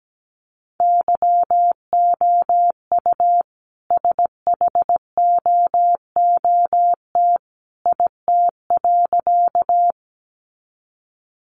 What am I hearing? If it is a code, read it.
Morse code, heard as a keyed tone, about 17 wpm, 700 Hz. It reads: YOU SHOOT IT.
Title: A Sound Of Thunder 17wpm